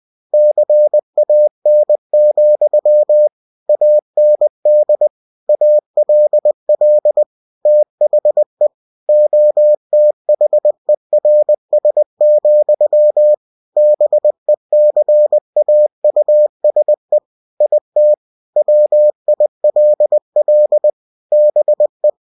Morse code, heard as a keyed tone, about 20 wpm, 600 Hz. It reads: CAN, AND ALL THE OTHERS, BECAUSE IT WILL BE
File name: cw-books / winnie-the-pooh / winnie-20wpm-page-0102